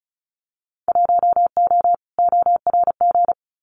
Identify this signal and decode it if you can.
Morse code, heard as a keyed tone, about 34 wpm, 700 Hz. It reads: 1O ORG
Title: Winter Field Day Exchanges 35wpm